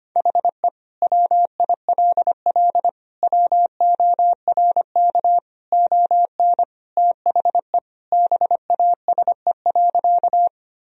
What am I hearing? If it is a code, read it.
Morse code, heard as a keyed tone, about 25 wpm, 700 Hz. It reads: HE WILL WORK ON THE BASE.
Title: Sentences from Top 1000 Words - Review Part 1 - 25wpm